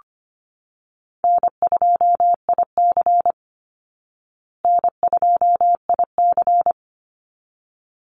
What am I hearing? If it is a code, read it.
Morse code, heard as a keyed tone, about 25 wpm, 700 Hz. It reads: N2IC N2IC
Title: Easy Call Signs Repeated 2x - 25wpm